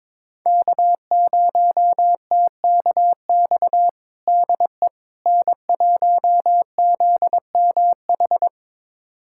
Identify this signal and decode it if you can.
Morse code, heard as a keyed tone, about 22 wpm, 700 Hz. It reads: K0TKX DE N1ZMH
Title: USA Callsign DE Callsign - 5 Characters - 22wpm